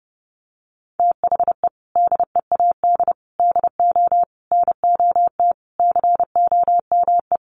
Morse code, heard as a keyed tone, about 30 wpm, 700 Hz. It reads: THE DEAD DO NOT COME
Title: Sentences from Top 1500 Words 30wpm